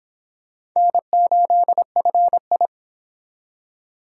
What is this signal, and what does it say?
Morse code, heard as a keyed tone, about 26 wpm, 700 Hz. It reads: N8FI